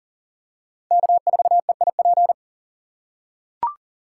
Morse code, heard as a keyed tone, about 40 wpm, 700 Hz. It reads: K4EIP
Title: USA Call Signs - 5 Characters - 4x Character Spacing - 40wpm